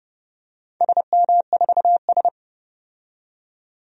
Morse code, heard as a keyed tone, about 32 wpm, 700 Hz. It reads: SM4S
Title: DXCC-100 Call Signs - 4 Characters - 30wpm